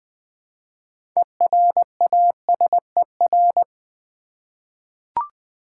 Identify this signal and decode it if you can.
Morse code, heard as a keyed tone, about 20 wpm, 690 Hz. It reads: ERASER
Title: Easy Words with Hints - 20wpm